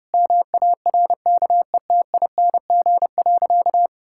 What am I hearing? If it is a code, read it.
Morse code, heard as a keyed tone, about 32 wpm, 700 Hz. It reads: MARKETING.
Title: Sentences from Top 1000 Words - Review Part 6 - 30wpm